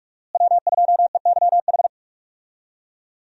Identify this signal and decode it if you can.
Morse code, heard as a keyed tone, about 46 wpm, 700 Hz. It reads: W2EYH
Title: USA Call Signs - 5 Characters - Repeated 2x - 45wpm